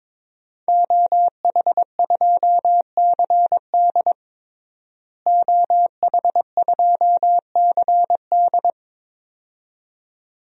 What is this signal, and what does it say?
Morse code, heard as a keyed tone, about 22 wpm, 700 Hz. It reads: OH2CD OH2CD
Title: DXCC-100 Call Signs - 5 Characters - Repeated 2x - 22wpm